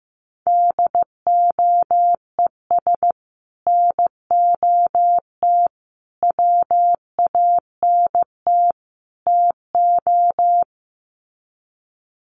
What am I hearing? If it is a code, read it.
Morse code, heard as a keyed tone, about 15 wpm, 700 Hz. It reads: DOES NOT WANT TO
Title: Sets of 4 Words 15wpm